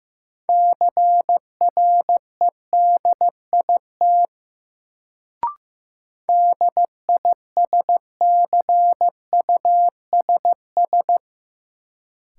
Morse code, heard as a keyed tone, about 15 wpm, 700 Hz. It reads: CREDIT DISCUSS
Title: Sentences from Top 1000 Words - New Word Drill - 15wpm